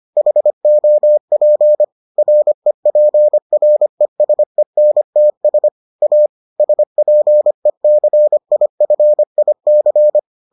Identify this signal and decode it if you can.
Morse code, heard as a keyed tone, about 25 wpm, 600 Hz. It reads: HOP REPRESENTS A SPECIFIC